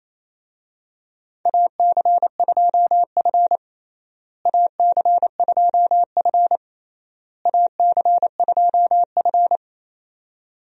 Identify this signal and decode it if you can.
Morse code, heard as a keyed tone, about 28 wpm, 700 Hz. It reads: AC2F AC2F AC2F
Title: USA Call Signs - 4 Characters - Repeated 3x - 28wpm